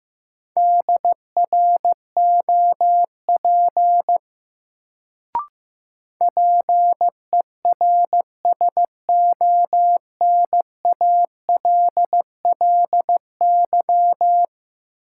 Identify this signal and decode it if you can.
Morse code, heard as a keyed tone, about 15 wpm, 700 Hz. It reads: DROP PERSONALLY
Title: Sentences from Top 1400 Words - New Word Drill - 15wpm